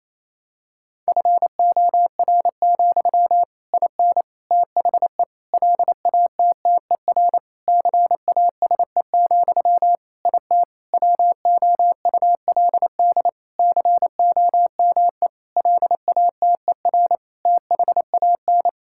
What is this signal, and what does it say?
Morse code, heard as a keyed tone, about 27 wpm, 700 Hz. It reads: FOR, IN THE LATTER CASE, IT WOULD COME LATER THAN
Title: Sentences from Top 1600 Words 28wpm